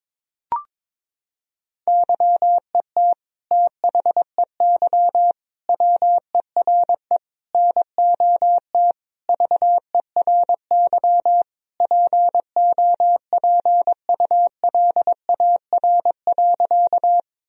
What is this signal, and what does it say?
Morse code, heard as a keyed tone, about 22 wpm, 700 Hz. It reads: YET THEY WERE NOT VERY POPULAR.
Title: Sentences from Top 1000 Words - Review Part 3 - 22wpm